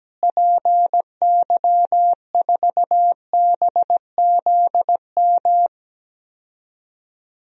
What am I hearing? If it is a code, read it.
Morse code, heard as a keyed tone, about 17 wpm, 700 Hz. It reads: PY4BZM
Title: DXCC-100 Call Signs - 6 Characters - Repeated 3x - 17wpm